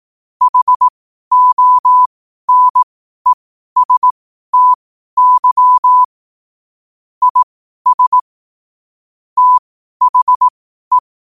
Morse code, heard as a keyed tone, about 18 wpm, spaced as for 12 wpm, 1000 Hz. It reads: HONESTY IS THE
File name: qotdfiles / 12wpm